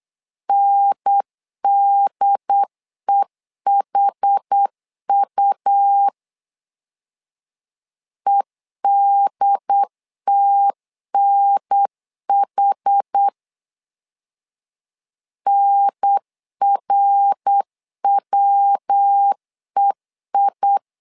Morse code, heard as a keyed tone, about 8 wpm, 800 Hz. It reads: NDEHU EDTNH NRWEI
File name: LEZIONI / 4 Q 8wpm 2732